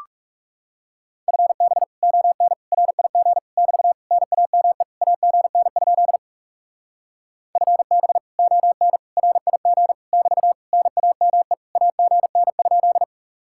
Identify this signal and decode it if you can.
Morse code, heard as a keyed tone, about 40 wpm, 700 Hz. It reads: FB ON RIG <BT> NAME AGN? FB ON RIG <BT> NAME AGN?